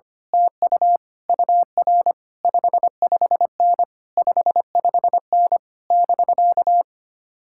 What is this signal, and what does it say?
Morse code, heard as a keyed tone, about 24 wpm, 700 Hz. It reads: TU UR 55N 55N <BK>
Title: Speed-Racing Common QSO Phrases - 25wpm